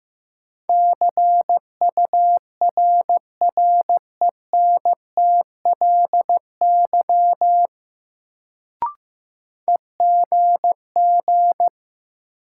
Morse code, heard as a keyed tone, about 15 wpm, 700 Hz. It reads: CURRENTLY EGG
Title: Sentences from Top 1000 Words - Review Part 7 - New Word Drill - 15wpm